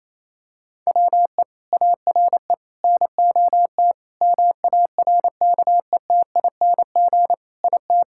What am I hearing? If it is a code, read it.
Morse code, heard as a keyed tone, about 28 wpm, 700 Hz. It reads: WE ARE NOT MARKETING IT
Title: Sentences from Top 1000 Words - Review Part 5 - 28wpm